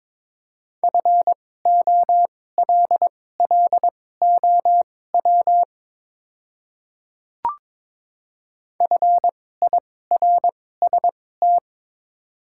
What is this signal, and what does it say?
Morse code, heard as a keyed tone, about 22 wpm, spaced as for 16 wpm, 700 Hz. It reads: FOLLOW FIRST
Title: 2x Character Spacing - Top 200 Words - 22wpm